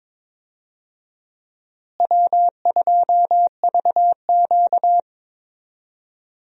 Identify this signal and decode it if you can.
Morse code, heard as a keyed tone, about 22 wpm, 700 Hz. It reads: W2VQ
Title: USA Call Signs - 4 Characters - Repeated 3x - 22wpm